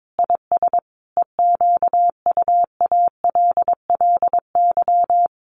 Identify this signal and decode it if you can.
Morse code, heard as a keyed tone, about 22 wpm, 700 Hz. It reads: IS EQUALLY
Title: Sentences from Top 1400 Words 22wpm